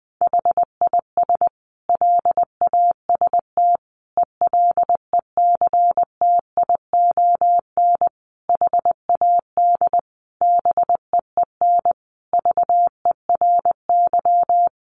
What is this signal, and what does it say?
Morse code, heard as a keyed tone, about 20 wpm, 700 Hz. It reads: HIS LAST ELECTION HAD BEEN VERY
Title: Sentences from Top 1500 Words 20wpm